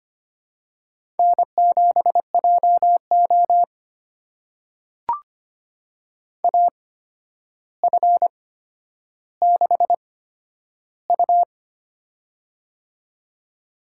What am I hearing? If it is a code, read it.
Morse code, heard as a keyed tone, about 25 wpm, spaced as for 11 wpm, 700 Hz. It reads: N7JO AF6U
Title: USA Call Signs - 4 Characters - 8x Character Spacing - 25wpm